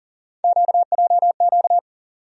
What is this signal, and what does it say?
Morse code, heard as a keyed tone, about 40 wpm, 700 Hz. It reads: QJQ
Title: USA Call Sign Postfixes - 3 Letters 40wpm